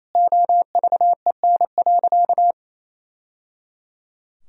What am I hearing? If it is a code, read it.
Morse code, heard as a keyed tone, about 28 wpm, 700 Hz. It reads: OVEN.
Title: Sentences from Top 1000 Words - Review Part 4 - 28wpm